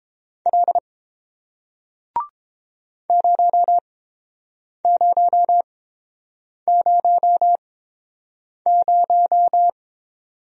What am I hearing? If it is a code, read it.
Morse code, heard as a keyed tone, about 31 wpm, spaced as for 7 wpm, 700 Hz. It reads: L 0000